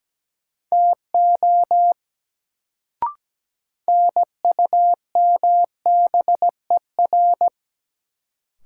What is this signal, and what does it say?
Morse code, heard as a keyed tone, about 17 wpm, 700 Hz. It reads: TO NUMBER
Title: Sentences from Top 100 Words - New Word Drill - 17wpm